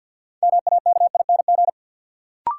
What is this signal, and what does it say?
Morse code, heard as a keyed tone, about 47 wpm, 700 Hz. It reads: MAKING
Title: Sentences from Top 1000 Words - Review Part 6 - New Word Drill - 50wpm